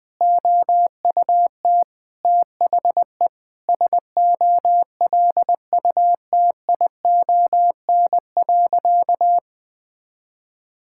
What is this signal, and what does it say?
Morse code, heard as a keyed tone, about 20 wpm, 700 Hz. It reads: OUT THE SOLUTION.
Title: Sentences with 8-Word Phrases - 20wpm